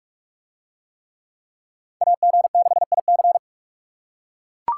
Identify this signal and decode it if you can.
Morse code, heard as a keyed tone, about 43 wpm, 700 Hz. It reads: AG6IC